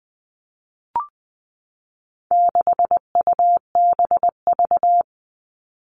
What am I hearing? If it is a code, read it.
Morse code, heard as a keyed tone, about 20 wpm, 700 Hz. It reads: 6UBV
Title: All Cut Numbers - 4-Digit 20wpm